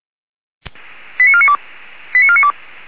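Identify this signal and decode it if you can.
Five-tone selective call (ZVEI), heard as quick runs of five tones, tones of about 70 ms. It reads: 89492 89492
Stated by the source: ZVEI five-tone call received by SDR.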